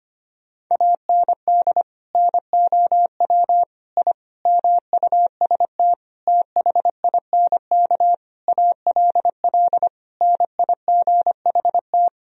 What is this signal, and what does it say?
Morse code, heard as a keyed tone, about 25 wpm, 700 Hz. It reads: AND NOW I MUST THINK ALL NIGHT
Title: Sentences from Top 1000 Words - Review Part 3 - 25wpm